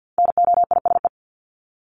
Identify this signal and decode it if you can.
Morse code, heard as a keyed tone, about 50 wpm, 700 Hz. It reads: NOISE
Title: Top 1501-1750 Words 50wpm